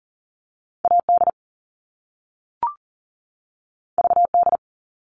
Morse code, heard as a keyed tone, about 42 wpm, 700 Hz. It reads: AD VD
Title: All Cut Numbers - 2-Digit 40wpm